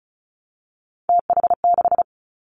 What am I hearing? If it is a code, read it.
Morse code, heard as a keyed tone, about 36 wpm, 700 Hz. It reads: TH6
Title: Three Challenging Letter-Number Combinations 35wpm